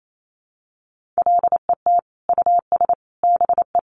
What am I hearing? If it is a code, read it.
Morse code, heard as a keyed tone, about 28 wpm, 700 Hz. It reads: LET US BE
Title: Sentences from Top 1400 Words 28wpm